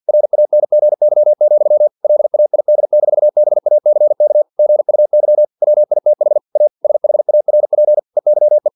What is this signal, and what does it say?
Morse code, heard as a keyed tone, about 50 wpm, 600 Hz. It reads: RANGY, LAID-BACK GUY WITH A SHARP EYE
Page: Never 49wpm